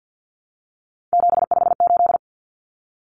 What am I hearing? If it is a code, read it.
Morse code, heard as a keyed tone, about 50 wpm, 700 Hz. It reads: 758
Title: Numbers - 3-Digit 50wpm